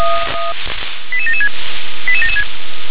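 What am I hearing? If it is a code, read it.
Five-tone selective call (ZVEI), heard as quick runs of five tones, tones of about 70 ms. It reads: 88776 88776
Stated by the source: ZVEI five-tone selective call 88799, received by SDR.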